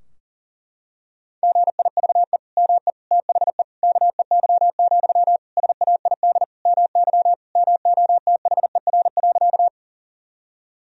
Morse code, heard as a keyed tone, about 40 wpm, 700 Hz. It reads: GIVE ME THE KEY, SAID MY MOTHER.
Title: Sentences from Top 1000 Words - Review Part 4 - 40wpm